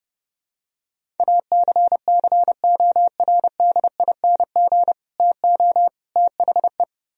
Morse code, heard as a keyed tone, about 30 wpm, 700 Hz. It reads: ACCORDING TO THE